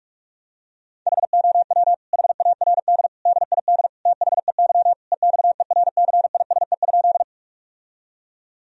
Morse code, heard as a keyed tone, about 41 wpm, 700 Hz. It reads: HOW HARD DID THEY EXERCISE?
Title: Sentences from Top 1000 Words - Review Part 4 - 45wpm